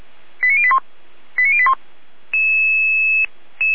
Five-tone selective call (ZVEI), heard as quick runs of five tones, tones of about 70 ms. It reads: 89081 89081